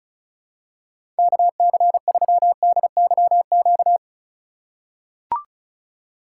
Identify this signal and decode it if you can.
Morse code, heard as a keyed tone, about 36 wpm, 700 Hz. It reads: KC3DYQ
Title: USA Call Signs - 6 Characters - 3x Character Spacing - 35wpm